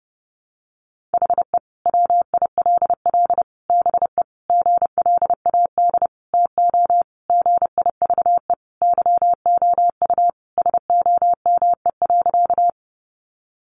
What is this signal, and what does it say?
Morse code, heard as a keyed tone, about 28 wpm, 700 Hz. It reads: HE WILL BE GLAD TO GIVE YOU SOME.
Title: Sentences from Top 800 Words 30wpm